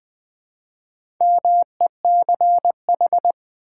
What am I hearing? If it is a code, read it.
Morse code, heard as a keyed tone, about 20 wpm, 700 Hz. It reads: MECH
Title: Build A Word - 4 Letters At A Time - 10-Letter Words - 20wpm